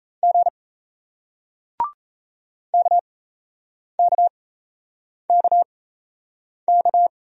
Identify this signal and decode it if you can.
Morse code, heard as a keyed tone, about 38 wpm, spaced as for 8 wpm, 700 Hz. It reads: G KKKK